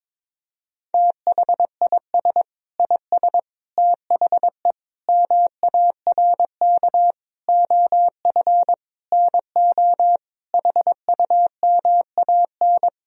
Morse code, heard as a keyed tone, about 22 wpm, 700 Hz. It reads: THIS IS THE MARK OF NO HUMAN
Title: Sentences from Top 1000 Words - Review Part 3 - 22wpm